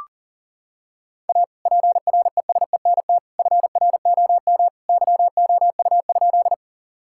Morse code, heard as a keyed tone, about 37 wpm, 700 Hz. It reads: A PRESENT FROM YOU?
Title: Sentences from Top 1000 Words - Review Part 5 - 40wpm